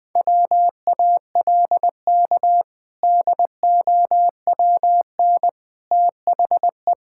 Morse code, heard as a keyed tone, about 20 wpm, 700 Hz. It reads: WALK DOWN THE
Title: Sentences with 2-Word Phrases - Spoken Beginning - 20wpm